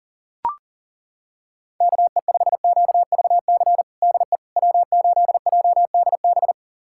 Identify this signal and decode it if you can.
Morse code, heard as a keyed tone, about 40 wpm, 700 Hz. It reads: KE5QVC DE W8JDB
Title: USA Callsign DE Callsign 40wpm